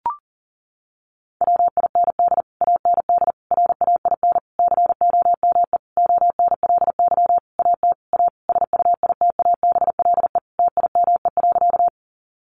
Morse code, heard as a keyed tone, about 40 wpm, 700 Hz. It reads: WIND AND RAIN COME ONLY AT A SUITABLE TIME.
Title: Sentences from Top 1400 Words 40wpm